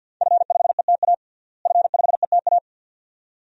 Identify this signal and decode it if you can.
Morse code, heard as a keyed tone, about 47 wpm, 700 Hz. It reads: F5ETA F5ETA
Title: DXCC-100 Call Signs - 5 Characters - Repeated 2x - 50wpm